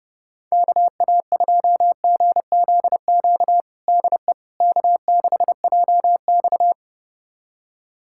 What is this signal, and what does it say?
Morse code, heard as a keyed tone, about 30 wpm, 700 Hz. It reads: KA2GZQ DE K6JX
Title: USA Callsign DE Callsign 30wpm